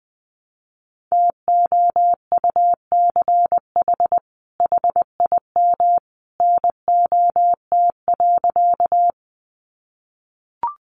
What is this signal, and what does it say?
Morse code, heard as a keyed tone, about 20 wpm, 700 Hz. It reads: TOUCH HIM NOT.
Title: Sentences from Top 1200 Words 20wpm